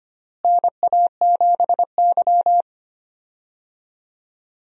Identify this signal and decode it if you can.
Morse code, heard as a keyed tone, about 25 wpm, 700 Hz. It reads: NA7Y
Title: USA Call Signs - 4 Characters - Repeated 3x - 25wpm